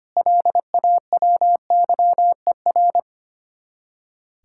Morse code, heard as a keyed tone, about 32 wpm, 700 Hz. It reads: LAWYER